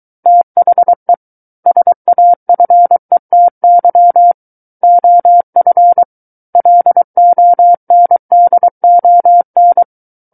Morse code, heard as a keyed tone, about 23 wpm, 700 Hz. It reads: THE SAFETY OF LONDON
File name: Book One / 14-Chapter 14 - 23 WPM